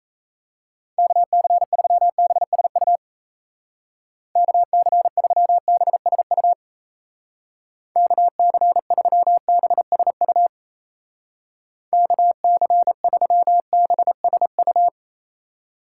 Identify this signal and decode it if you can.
Morse code, heard as a keyed tone, about 35 wpm, 700 Hz. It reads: KC3BSU KC3BSU KC3BSU KC3BSU
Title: Speed Racing - USA 2x3 Call Signs 28wpm